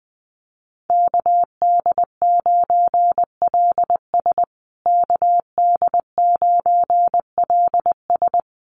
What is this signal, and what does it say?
Morse code, heard as a keyed tone, about 20 wpm, 700 Hz. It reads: KD9LS KD9LS
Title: USA Call Signs Repeated 2x 20wpm